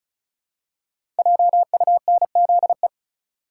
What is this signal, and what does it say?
Morse code, heard as a keyed tone, about 36 wpm, 700 Hz. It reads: JUNZE
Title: Code Groups - 5 Letters - 35wpm